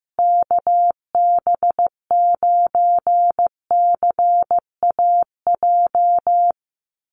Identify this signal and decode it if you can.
Morse code, heard as a keyed tone, about 14 wpm, 700 Hz. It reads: KB9CAJ
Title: USA Call Signs Repeated 2x 15wpm